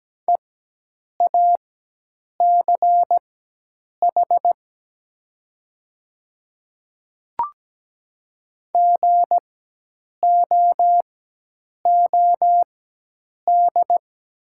Morse code, heard as a keyed tone, about 17 wpm, spaced as for 8 wpm, 700 Hz. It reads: EACH GOOD
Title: 4x Character Spacing - Top 500 Words - 17wpm